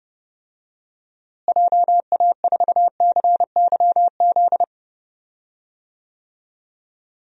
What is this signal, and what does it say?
Morse code, heard as a keyed tone, about 30 wpm, 700 Hz. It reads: JA4CYZ
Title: DXCC-100 Call Signs - 6 Characters - Repeated 3x - 30wpm